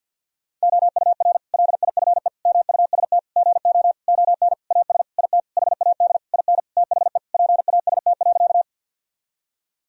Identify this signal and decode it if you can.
Morse code, heard as a keyed tone, about 52 wpm, 700 Hz. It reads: OUR LIFE MUST GO ON AS IT HAD IN THE PAST.
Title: Sentences from Top 1000 Words - Review Part 3 - 50wpm